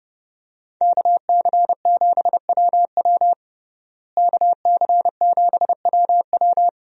Morse code, heard as a keyed tone, about 32 wpm, 700 Hz. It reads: KC7WW KC7WW
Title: USA Call Signs - 5 Characters - Repeated 3x - 30wpm